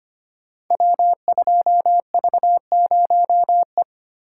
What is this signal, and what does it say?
Morse code, heard as a keyed tone, about 25 wpm, 700 Hz. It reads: W2V0E
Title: Code Groups - 5 Letter:Numbers - 25wpm